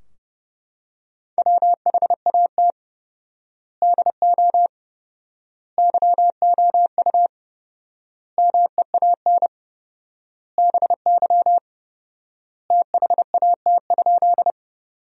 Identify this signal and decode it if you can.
Morse code, heard as a keyed tone, about 30 wpm, 700 Hz. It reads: WHAT DO YOU MEAN BY THAT?
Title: 4x Word-Spacing - Sentences from Top 200 Words - 30wpm